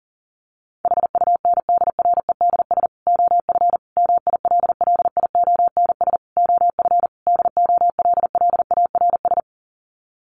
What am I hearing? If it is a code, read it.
Morse code, heard as a keyed tone, about 40 wpm, 700 Hz. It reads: HUNDREDS OF MILLIONS OF DOLLARS
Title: Sets of 5 Words 40wpm